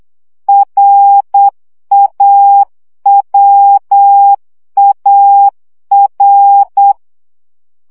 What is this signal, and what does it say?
Morse code, heard as a keyed tone, about 8 wpm, 800 Hz. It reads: RAWAR
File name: CW gyakorlas a r w 7 baud